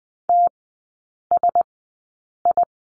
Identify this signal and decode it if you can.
Morse code, heard as a keyed tone, about 20 wpm, spaced as for 8 wpm, 700 Hz. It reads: TSI
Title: 041 Lesson 8e - Number 1 - Three Characters 20wpm